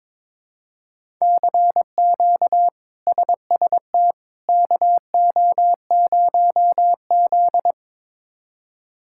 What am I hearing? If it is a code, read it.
Morse code, heard as a keyed tone, about 21 wpm, 700 Hz. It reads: CQ SST KO0Z
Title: K1USN SST QSO Practice - 22wpm